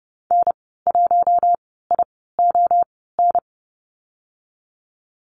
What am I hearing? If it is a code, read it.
Morse code, heard as a keyed tone, about 30 wpm, spaced as for 17 wpm, 700 Hz. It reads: N1ION
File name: Lesson 14g - Number 2 - Call Signs 30wpm